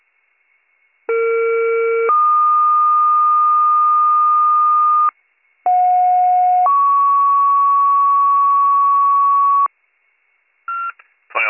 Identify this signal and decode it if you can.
Two-tone page, heard as a one-second tone, then a three-second tone, about 460 then 1220 Hz, about 730 then 1120 Hz, with speech. It is heard from 1.1 to 9.7 s.